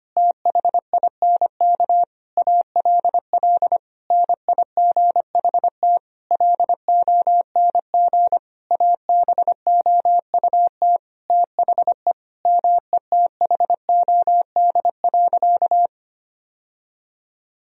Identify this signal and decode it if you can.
Morse code, heard as a keyed tone, about 25 wpm, 700 Hz. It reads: THINK ALL NIGHT LONG ABOUT THE METHOD.
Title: Sentences from Top 1000 Words - Review Part 3 - 25wpm